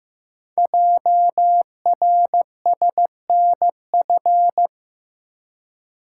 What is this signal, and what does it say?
Morse code, heard as a keyed tone, about 15 wpm, 700 Hz. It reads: JRSNF
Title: Code Groups - 5 Letters - 15wpm